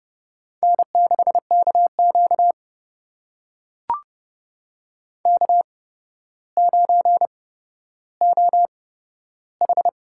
Morse code, heard as a keyed tone, about 31 wpm, spaced as for 13 wpm, 700 Hz. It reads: N6KQ K9OH
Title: USA Call Signs - 4 Characters - 8x Character Spacing - 30wpm